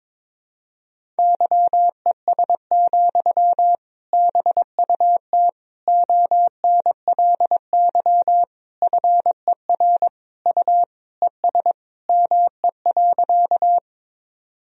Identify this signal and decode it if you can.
Morse code, heard as a keyed tone, about 22 wpm, 700 Hz. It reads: YES, BUT ONLY FER U ES ME.
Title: Sentences from Top 500 Words and Abbreviations 22wpm